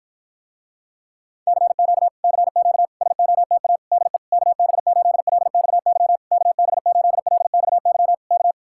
Morse code, heard as a keyed tone, about 54 wpm, 700 Hz. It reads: CQ CQ SOTA DE KB8LXY KB8LXY K